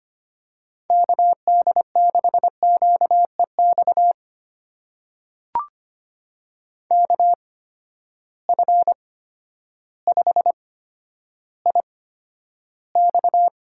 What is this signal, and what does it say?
Morse code, heard as a keyed tone, about 25 wpm, spaced as for 11 wpm, 700 Hz. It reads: KD6QEX KF5IX
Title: USA Call Signs - 6 Characters - 8x Character Spacing - 25wpm